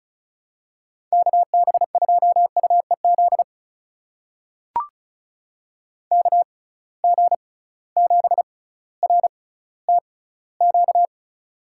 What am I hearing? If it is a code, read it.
Morse code, heard as a keyed tone, about 36 wpm, spaced as for 18 wpm, 690 Hz. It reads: KB2UEZ KG7RTQ
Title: USA Call Signs - 6 Characters - 6x Character Spacing - 35wpm